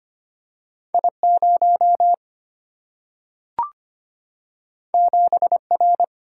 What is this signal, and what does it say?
Morse code, heard as a keyed tone, about 25 wpm, 700 Hz. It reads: I 0 7 R
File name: Two Letter-Number Combinations - 25wpm